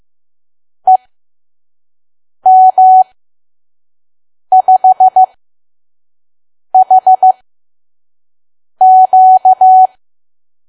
Morse code, heard as a keyed tone, about 15 wpm, spaced as for 5 wpm, 750 Hz. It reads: EM5HQ
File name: Archive / 5 WPM files / PM